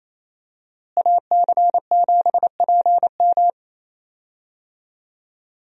Morse code, heard as a keyed tone, about 28 wpm, 700 Hz. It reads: AC7PM
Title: USA Call Signs - 5 Characters - Repeated 2x - 28wpm